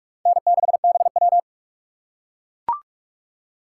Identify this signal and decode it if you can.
Morse code, heard as a keyed tone, about 44 wpm, 690 Hz. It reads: N6BW
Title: USA Call Signs - 4 Characters - 3x Character Spacing - 45wpm